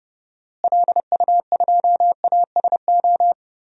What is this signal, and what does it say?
Morse code, heard as a keyed tone, about 30 wpm, 700 Hz. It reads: LU2ASO
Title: DXCC-100 Call Signs - 6 Characters - Repeated 3x - 30wpm